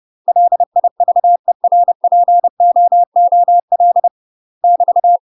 Morse code, heard as a keyed tone, about 30 wpm, 700 Hz. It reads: LIVERPOOL <BT>
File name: UK MASTER QSOs TRAINING / 30WPM / UK MASTER QSOs 0029